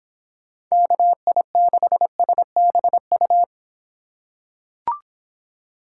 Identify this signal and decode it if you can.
Morse code, heard as a keyed tone, about 26 wpm, 700 Hz. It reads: KI6SBU